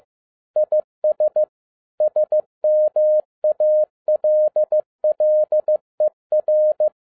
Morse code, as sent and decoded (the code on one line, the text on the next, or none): .. ... / ... -- .- .-.. .-.. .-.
IS SMALLER